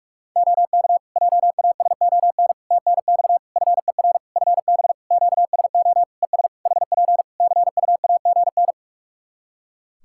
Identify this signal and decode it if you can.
Morse code, heard as a keyed tone, about 42 wpm, 700 Hz. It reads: OK JASON TNX FER FB QSO ES HP CUAGN
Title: Practice QSO - Exchange 4 - Ending - 45wpm